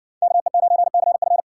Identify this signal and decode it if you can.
Morse code, heard as a keyed tone, about 59 wpm, 700 Hz. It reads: KE9CF